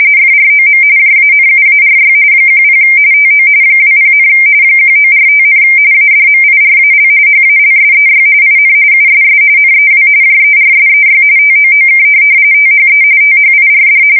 RTTY, about 45.45 baud. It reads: FROM JANUARY 10 TO 20, 1945, ADMIRAL HALSEY IN USS NEW JERSEY TOOK THE THIRD
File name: fsk170 / USS-New-Jersey